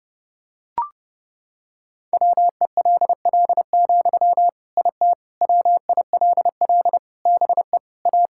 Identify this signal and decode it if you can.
Morse code, heard as a keyed tone, about 32 wpm, 700 Hz. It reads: WELL, IT WILL BE A
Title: Sentences from Top 1000 Words - Review Part 6 - 30wpm